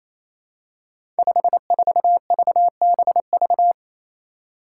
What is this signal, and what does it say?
Morse code, heard as a keyed tone, about 27 wpm, 700 Hz. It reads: 54VBV
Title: Five Challenging Letter-Number Combinations 28wpm